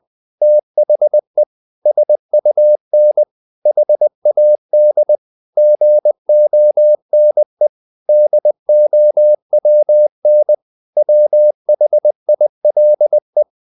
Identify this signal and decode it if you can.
Morse code, heard as a keyed tone, about 20 wpm, 600 Hz. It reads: THE SUN HAD GONE DOWN WHILE
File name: the old man and the sea part 2 20wpm